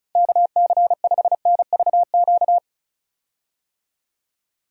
Morse code, heard as a keyed tone, about 34 wpm, 700 Hz. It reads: KC5NVQ
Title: USA Call Signs - 6 Characters - Repeated 2x - 35wpm